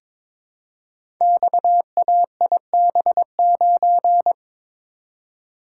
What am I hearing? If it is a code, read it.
Morse code, heard as a keyed tone, about 23 wpm, 690 Hz. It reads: XAIB9
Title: Code Groups - 5 Letter:Numbers - 22wpm